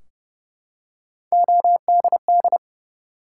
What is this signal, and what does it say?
Morse code, heard as a keyed tone, about 30 wpm, 700 Hz. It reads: ODD
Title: Three Letter Words - 30wpm